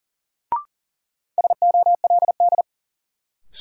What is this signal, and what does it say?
Morse code, heard as a keyed tone, about 40 wpm, 700 Hz. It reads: SOLD
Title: Sentences from Top 1400 Words - New Word Drill Alt - 40wpm